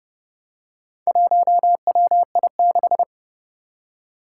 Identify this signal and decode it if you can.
Morse code, heard as a keyed tone, about 30 wpm, 700 Hz. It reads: 1WI6
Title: Four Challenging Letter-Number Combinations 30wpm